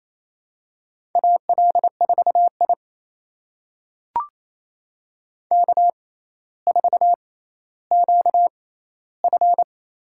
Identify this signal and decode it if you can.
Morse code, heard as a keyed tone, about 28 wpm, spaced as for 14 wpm, 700 Hz. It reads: AL4I K4QF
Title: USA Call Signs - 4 Characters - 6x Character Spacing - 28wpm